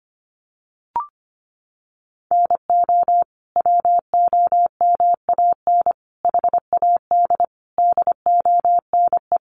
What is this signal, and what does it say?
Morse code, heard as a keyed tone, about 25 wpm, 700 Hz. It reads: NO WOMAN HAD DONE
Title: Sentences from Top 1100 Words 25wpm